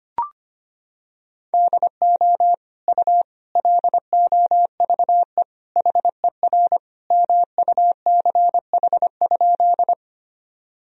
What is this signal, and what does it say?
Morse code, heard as a keyed tone, about 25 wpm, 700 Hz. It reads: DO U LOVE HER MUCH?
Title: Sentences from Top 400 Words and Abbreviations 25wpm